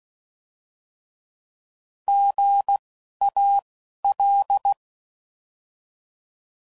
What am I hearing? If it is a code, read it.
Morse code, heard as a keyed tone, about 16 wpm, spaced as for 11 wpm, 800 Hz. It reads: GAL